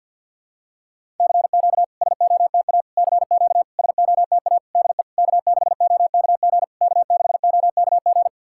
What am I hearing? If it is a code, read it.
Morse code, heard as a keyed tone, about 50 wpm, 690 Hz. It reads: CQ SOTA CQ SOTA DE K6OKG K6OKG
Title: SOTA QSO Practice 50wpm